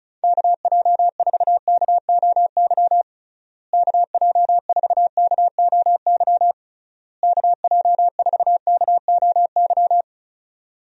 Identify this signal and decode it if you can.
Morse code, heard as a keyed tone, about 34 wpm, 700 Hz. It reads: KJ4KOY KJ4KOY KJ4KOY
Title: USA Call Signs - 6 Characters - Repeated 3x - 35wpm